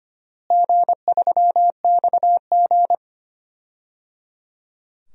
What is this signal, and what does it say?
Morse code, heard as a keyed tone, about 25 wpm, 700 Hz. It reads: G3XG
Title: DXCC-100 Call Signs - 4 Characters - Repeated 3x - 25wpm